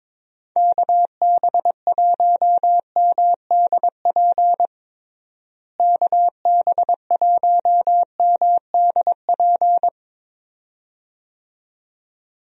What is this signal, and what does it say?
Morse code, heard as a keyed tone, about 22 wpm, 700 Hz. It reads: KB1MDP KB1MDP
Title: USA Call Signs - 6 Characters - Repeated 2x - 22wpm